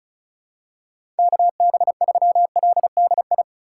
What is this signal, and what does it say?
Morse code, heard as a keyed tone, about 34 wpm, 700 Hz. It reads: KB3LDI
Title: USA Call Signs - 6 Characters - 4x Character Spacing - 35wpm